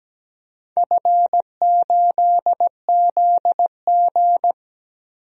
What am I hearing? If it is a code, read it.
Morse code, heard as a keyed tone, about 17 wpm, 700 Hz. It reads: F8ZG
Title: DXCC-100 Call Signs - 4 Characters - 17wpm